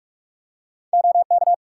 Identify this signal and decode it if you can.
Morse code, heard as a keyed tone, about 41 wpm, 700 Hz. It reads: OK
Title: Practice QSO - Exchange 2 - Info - 45wpm